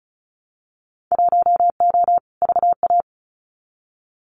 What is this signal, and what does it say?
Morse code, heard as a keyed tone, about 34 wpm, 700 Hz. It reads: 1O VA
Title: Winter Field Day Exchanges 35wpm